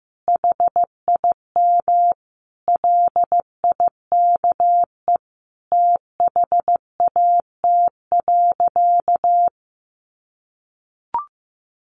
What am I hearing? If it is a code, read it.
Morse code, heard as a keyed tone, about 15 wpm, 700 Hz. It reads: HIM LIKE THAT.
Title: Sentences from Top 1300 Words 15wpm